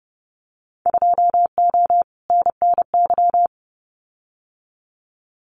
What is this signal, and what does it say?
Morse code, heard as a keyed tone, about 30 wpm, 700 Hz. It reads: 2O NNY
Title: Winter Field Day Exchanges 30wpm